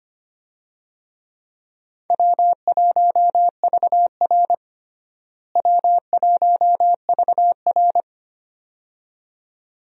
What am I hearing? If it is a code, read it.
Morse code, heard as a keyed tone, about 25 wpm, 700 Hz. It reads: W1VR W1VR
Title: USA Call Signs - 4 Characters - Repeated 2x - 25wpm